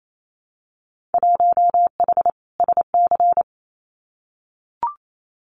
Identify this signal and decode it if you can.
Morse code, heard as a keyed tone, about 28 wpm, 700 Hz. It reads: 1H SC
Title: Winter Field Day Exchanges 28wpm